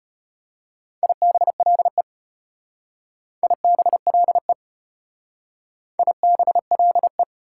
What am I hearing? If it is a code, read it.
Morse code, heard as a keyed tone, about 33 wpm, 700 Hz. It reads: IBLE IBLE IBLE